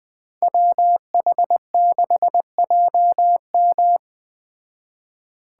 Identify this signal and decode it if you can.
Morse code, heard as a keyed tone, about 20 wpm, 700 Hz. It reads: WH6JM
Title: Speed Racing - USA Call Signs 17wpm